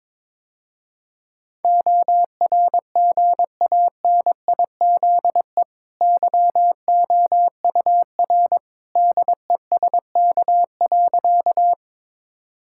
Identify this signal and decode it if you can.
Morse code, heard as a keyed tone, about 22 wpm, 690 Hz. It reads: ORGANIZE YOUR DESK.
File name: practice / 22wpm / Easy Sentences with Hints - 22wpm